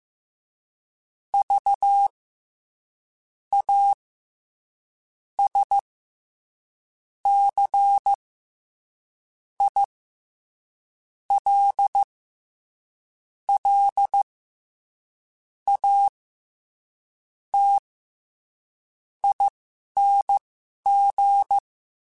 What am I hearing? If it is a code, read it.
Morse code, heard as a keyed tone, about 15 wpm, spaced as for 11 wpm, 800 Hz. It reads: V A S C I L L A T ING
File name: Special INGs-mp3s / Special INGs-mp3s / SlowToFaster ing @10